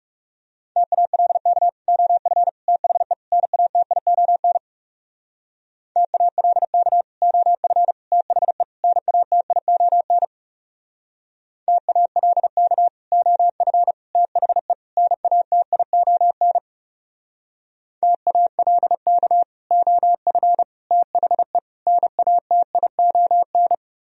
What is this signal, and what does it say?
Morse code, heard as a keyed tone, about 37 wpm, 700 Hz. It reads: TALK OF THE NATION TALK OF THE NATION TALK OF THE NATION TALK OF THE NATION
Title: Speed-Racing - Sets of 4 Words 30wpm